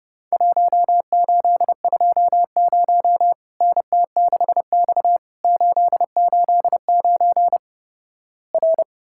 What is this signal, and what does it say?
Morse code, heard as a keyed tone, about 32 wpm, 700 Hz. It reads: 1820 NT6X 889 R
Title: Worked All Europe DX Contest - QTC Exchange - 30wpm